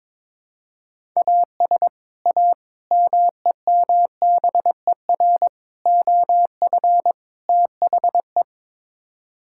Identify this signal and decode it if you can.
Morse code, heard as a keyed tone, about 22 wpm, 700 Hz. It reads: AS A MEMBER OF THE
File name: practice / 22wpm / Build A Phrase - 5-Word Phrases - 22wpm